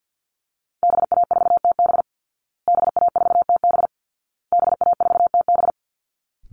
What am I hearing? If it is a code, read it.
Morse code, heard as a keyed tone, about 50 wpm, 700 Hz. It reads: BA4TB BA4TB BA4TB
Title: Call Signs Repeated 2x - 50wpm